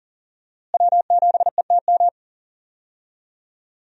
Morse code, heard as a keyed tone, about 40 wpm, 700 Hz. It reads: W7ETM
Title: USA Call Signs - 5 Characters - Repeated 2x - 40wpm